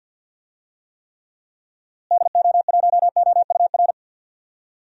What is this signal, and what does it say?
Morse code, heard as a keyed tone, about 52 wpm, 700 Hz. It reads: DO1OUR
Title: DXCC-100 Call Signs - 6 Characters - Repeated 2x - 50wpm